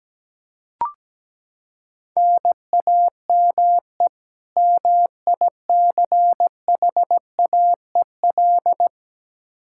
Morse code, heard as a keyed tone, about 17 wpm, 700 Hz. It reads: NAME MICHAEL
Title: Practice QSO - Exchange 1 - Beginning - 17wpm